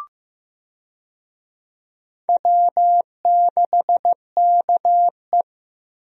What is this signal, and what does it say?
Morse code, heard as a keyed tone, about 15 wpm, 700 Hz. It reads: W6KE
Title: USA Call Signs - 4 Characters - Repeated 3x - 15wpm